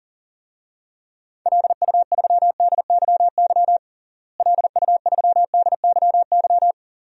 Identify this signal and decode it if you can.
Morse code, heard as a keyed tone, about 40 wpm, 700 Hz. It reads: LU3DYY LU3DYY
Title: DXCC-100 Call Signs - 6 Characters - Repeated 2x - 40wpm